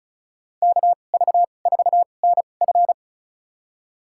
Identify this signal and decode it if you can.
Morse code, heard as a keyed tone, about 36 wpm, spaced as for 25 wpm, 700 Hz. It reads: KV4NF
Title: USA Call Signs - 5 Characters - 2x Character Spacing - 35wpm